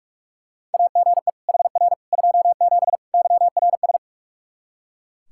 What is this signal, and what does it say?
Morse code, heard as a keyed tone, about 44 wpm, 700 Hz. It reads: AGE HR 27 YRS
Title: Practice QSO - Exchange 3 - More Info - 45wpm